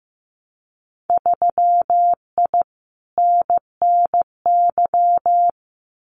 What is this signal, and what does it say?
Morse code, heard as a keyed tone, about 15 wpm, 700 Hz. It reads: 3I NNY
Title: Winter Field Day Exchanges 15wpm